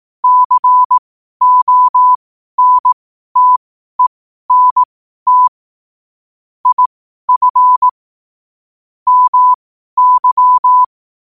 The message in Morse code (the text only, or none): CONTENT IF MY